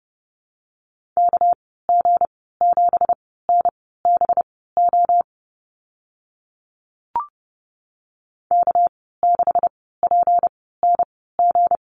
Morse code, heard as a keyed tone, about 32 wpm, spaced as for 17 wpm, 700 Hz. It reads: KG7NBO K6PNG
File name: Lesson 37g - Letter x - Call Signs 30wpm